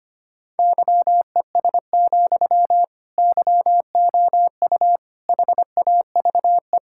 Morse code, heard as a keyed tone, about 25 wpm, 700 Hz. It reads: YES, YOU HAVE
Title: Sentences from Top 1600 Words 25wpm